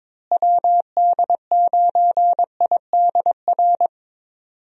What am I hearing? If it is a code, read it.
Morse code, heard as a keyed tone, about 22 wpm, 700 Hz. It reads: WD9IDR
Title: USA Call Signs - 6 Characters - Repeated 2x - 22wpm